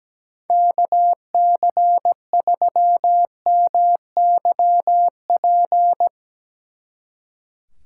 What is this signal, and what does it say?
Morse code, heard as a keyed tone, about 18 wpm, 700 Hz. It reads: KC3MYP
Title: Speed Racing - USA 2x3 Call Signs 17wpm